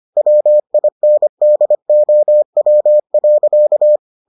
Morse code, heard as a keyed tone, about 25 wpm, 600 Hz. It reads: WINDOW.